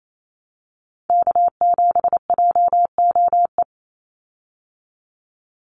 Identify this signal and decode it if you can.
Morse code, heard as a keyed tone, about 28 wpm, 700 Hz. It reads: K7JOE
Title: Call Signs Repeated 2x - 28wpm